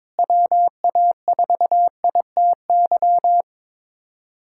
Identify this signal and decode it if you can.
Morse code, heard as a keyed tone, about 22 wpm, 700 Hz. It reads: WA4ITY
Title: USA Callsign DE Callsign - 6 Characters - 22wpm